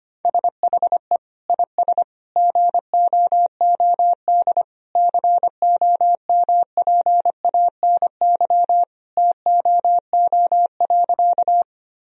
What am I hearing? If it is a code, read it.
Morse code, heard as a keyed tone, about 25 wpm, 700 Hz. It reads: SHE IS GOOD COMPANY TOO.
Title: Sentences from Top 1000 Words - Review Part 3 - 25wpm